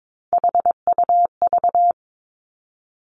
Morse code, heard as a keyed tone, about 22 wpm, 700 Hz. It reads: HUV
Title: Three Challenging Letter-Number Combinations 22wpm